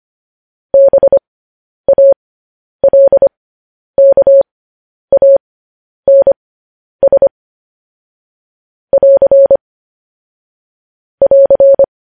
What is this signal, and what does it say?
Morse code, heard as a keyed tone, about 25 wpm, spaced as for 10 wpm, 550 Hz. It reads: BALKANS <AR> <AR>
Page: News Headlines in Morse Code at 10 WPM